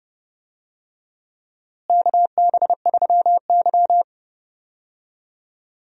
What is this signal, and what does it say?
Morse code, heard as a keyed tone, about 30 wpm, 700 Hz. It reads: KB3Y
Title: USA Call Signs - 4 Characters - Repeated 3x - 30wpm